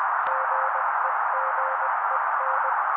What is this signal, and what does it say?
Morse code, heard as a keyed tone, about 20 wpm, 530 Hz. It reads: GEGEN
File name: daily / 18wpm noise